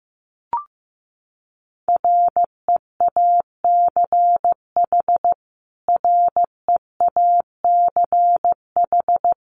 Morse code, heard as a keyed tone, about 15 wpm, 700 Hz. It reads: REACH REACH
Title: Top 500 Words Repeated 15wpm